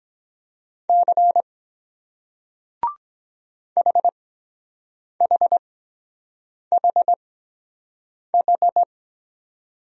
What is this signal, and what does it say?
Morse code, heard as a keyed tone, about 23 wpm, spaced as for 7 wpm, 700 Hz. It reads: C HHHH